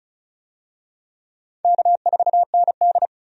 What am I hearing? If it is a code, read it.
Morse code, heard as a keyed tone, about 36 wpm, 700 Hz. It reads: K4ND
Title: USA Call Signs - 4 Characters - 35wpm